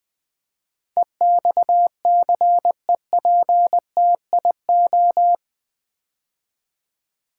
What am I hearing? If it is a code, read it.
Morse code, heard as a keyed tone, about 20 wpm, 700 Hz. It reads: EXCEPTIO
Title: Build A Word - 4 Letters At A Time - 11-Letter Words - 20wpm